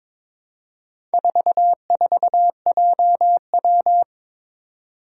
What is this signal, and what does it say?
Morse code, heard as a keyed tone, about 23 wpm, 700 Hz. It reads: 44JW
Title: Four Challenging Letter-Number Combinations 22wpm